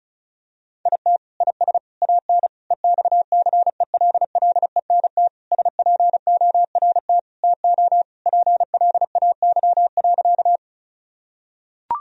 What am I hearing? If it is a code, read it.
Morse code, heard as a keyed tone, about 36 wpm, 700 Hz. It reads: IT IS AN EXCELLENT SPORT TO PLAY.
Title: Sentences from Top 1000 Words - Review Part 4 - 35wpm